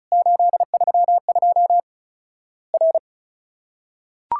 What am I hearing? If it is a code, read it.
Morse code, heard as a keyed tone, about 36 wpm, 700 Hz. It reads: 832 R
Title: Worked All Europe DX Contest - QTC Exchange - 35wpm